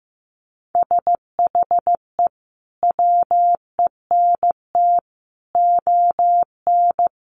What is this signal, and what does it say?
Morse code, heard as a keyed tone, about 15 wpm, 700 Hz. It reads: SHE WENT ON
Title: Sentences from Top 800 Words 15wpm